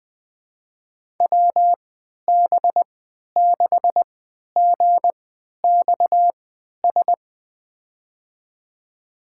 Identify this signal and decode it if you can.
Morse code, heard as a keyed tone, about 21 wpm, spaced as for 11 wpm, 700 Hz. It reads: WB6GXS
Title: USA Call Signs - 6 Characters - 3x Character Spacing - 20wpm